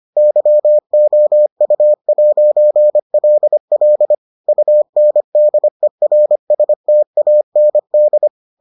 Morse code, heard as a keyed tone, about 25 wpm, 600 Hz. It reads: YOU'LL UNDERSTAND